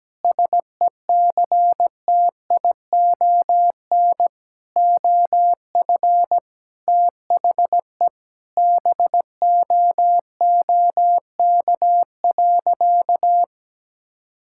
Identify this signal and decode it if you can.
Morse code, heard as a keyed tone, about 17 wpm, 700 Hz. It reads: SECTION OF THE BOOK.
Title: Sentences from Top 1000 Words - Review Part 5 - 17wpm